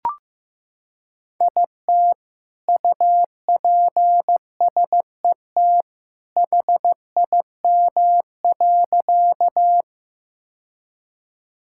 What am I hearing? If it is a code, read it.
Morse code, heard as a keyed tone, about 15 wpm, 700 Hz. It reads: IT UPSET HIM.
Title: Sentences from Top 1000 Words - Review Part 5 - 15wpm